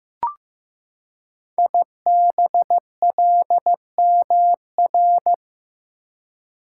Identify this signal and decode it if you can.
Morse code, heard as a keyed tone, about 15 wpm, 700 Hz. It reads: IBLMR
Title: Code Groups - 5 Letters - 15wpm